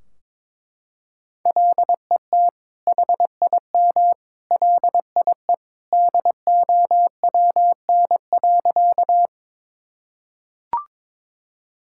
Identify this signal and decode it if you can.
Morse code, heard as a keyed tone, about 22 wpm, 700 Hz. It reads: LET HIM LIE DOWN.
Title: Sentences from Top 1600 Words 22wpm